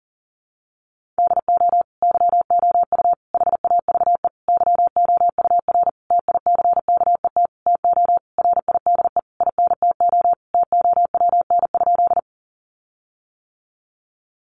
Morse code, heard as a keyed tone, about 40 wpm, 700 Hz. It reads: DO YOU HAVE YOUR TICKET TO RIDE INTO TOWN?
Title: Sentences from Top 1400 Words 40wpm